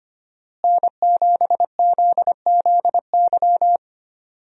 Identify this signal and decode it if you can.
Morse code, heard as a keyed tone, about 25 wpm, 700 Hz. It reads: N7ZZY
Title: USA Call Signs - 5 Characters - Repeated 3x - 25wpm